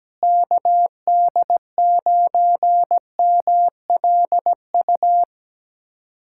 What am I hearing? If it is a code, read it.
Morse code, heard as a keyed tone, about 17 wpm, 700 Hz. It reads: KD9MLU
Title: USA Callsign DE Callsign - 6 Characters - 17wpm